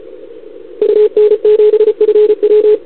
Morse code, heard as a keyed tone, about 33 wpm, 440 Hz. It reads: UN7FW
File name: CQWWCW2021 / OL3Z